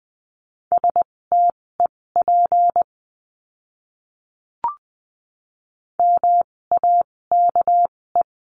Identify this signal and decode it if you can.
Morse code, heard as a keyed tone, about 20 wpm, spaced as for 16 wpm, 700 Hz. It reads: STEP MAKE
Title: 256 Lesson 42a - Words 12-20wpm